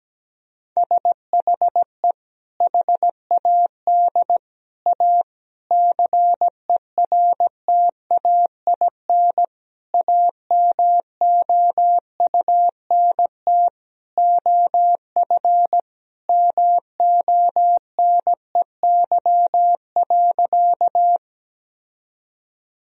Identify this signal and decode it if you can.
Morse code, heard as a keyed tone, about 17 wpm, 700 Hz. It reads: SHE HAD A CERTAIN AMOUNT OF MONEY.
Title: Sentences from Top 1000 Words - Review Part 4 - 17wpm